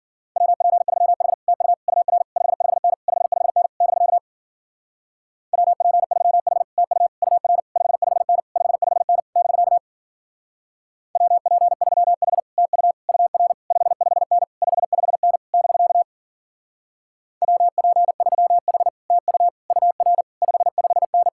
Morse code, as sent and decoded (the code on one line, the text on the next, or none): .-- .--. ...-- .... / - ..- / ..- .-. / ..... ..... -. / ..... ..... -. / -...-.- / .-- .--. ...-- .... / - ..- / ..- .-. / ..... ..... -. / ..... ..... -. / -...-.- / .-- .--. ...-- .... / - ..- / ..- .-. / ..... ..... -. / ..... ..... -. / -...-.- / .-- .--. ...-- .... / - ..- / ..- .-. / ..... ..... -.
WP3H TU UR 55N 55N <BK> WP3H TU UR 55N 55N <BK> WP3H TU UR 55N 55N <BK> WP3H TU UR 55N